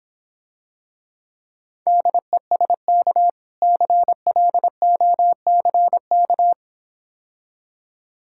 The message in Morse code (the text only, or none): DESK CLOCK